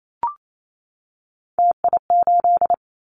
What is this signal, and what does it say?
Morse code, heard as a keyed tone, about 28 wpm, 700 Hz. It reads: TI8
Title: Callsign Prefixes for Top 100 DXCC - 28wpm